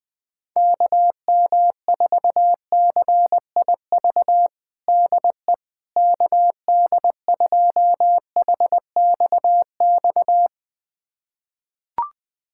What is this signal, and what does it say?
Morse code, heard as a keyed tone, about 21 wpm, 700 Hz. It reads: KM4CIV DE KD2HXX
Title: USA Callsign DE Callsign 20wpm